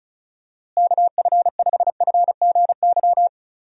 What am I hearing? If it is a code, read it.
Morse code, heard as a keyed tone, about 34 wpm, 700 Hz. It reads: KF5FGY
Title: USA Call Signs - 6 Characters - Repeated 2x - 35wpm